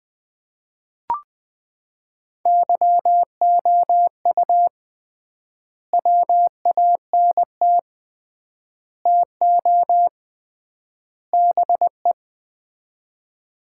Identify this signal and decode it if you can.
Morse code, heard as a keyed tone, about 20 wpm, 700 Hz. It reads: YOU WANT TO BE
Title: 3x Word-Spacing - Sets of 4 Words - 20wpm